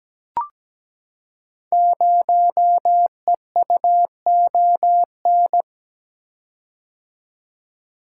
Morse code, heard as a keyed tone, about 17 wpm, 700 Hz. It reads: 0EUON
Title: Code Groups - 5 Letter:Numbers - 17wpm